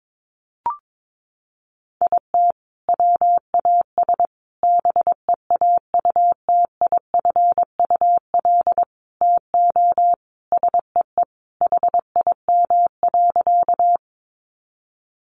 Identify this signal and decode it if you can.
Morse code, heard as a keyed tone, about 22 wpm, 700 Hz. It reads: IT WAS BEAUTIFUL TO SEE HIM.
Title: Sentences from Top 900 Words 22wpm